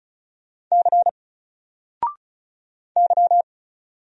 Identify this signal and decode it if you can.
Morse code, heard as a keyed tone, about 36 wpm, 700 Hz. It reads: C Y
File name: Single-Letter-Number-35wpm